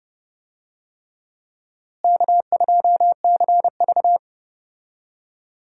Morse code, heard as a keyed tone, about 30 wpm, 700 Hz. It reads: K2CV
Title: USA Call Signs - 4 Characters - Repeated 3x - 30wpm